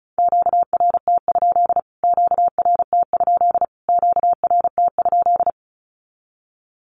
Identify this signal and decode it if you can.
Morse code, heard as a keyed tone, about 34 wpm, 700 Hz. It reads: QRT? QRT? QRT?
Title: Q-Codes Repeated 3x 35wpm